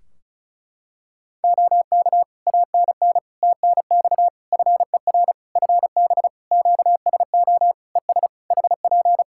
Morse code, heard as a keyed tone, about 36 wpm, 700 Hz. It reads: OK ANN TNX FER FB QSO ES HP
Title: Practice QSO - Exchange 4 - Ending - 35wpm